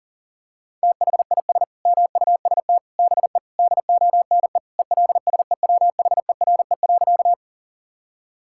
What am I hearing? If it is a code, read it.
Morse code, heard as a keyed tone, about 40 wpm, 700 Hz. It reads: THIS MUST BE DONE ELSEWHERE.
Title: Sentences from Top 1600 Words 40wpm